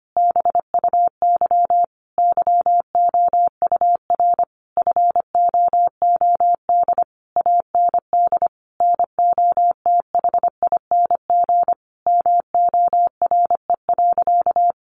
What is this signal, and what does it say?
Morse code, heard as a keyed tone, about 25 wpm, 700 Hz. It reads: BUY YOUR FOOD AND NOTHING MORE.
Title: Sentences from Top 800 Words 25wpm